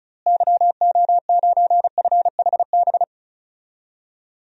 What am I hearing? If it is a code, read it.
Morse code, heard as a keyed tone, about 34 wpm, 700 Hz. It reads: YO9FHB
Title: DXCC-100 Call Signs - 6 Characters - Repeated 2x - 35wpm